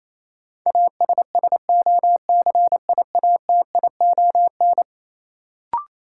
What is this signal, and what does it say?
Morse code, heard as a keyed tone, about 28 wpm, 700 Hz. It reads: ASSOCIATION
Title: Sentences from Top 1100 Words - New Word Drill Alt - 28wpm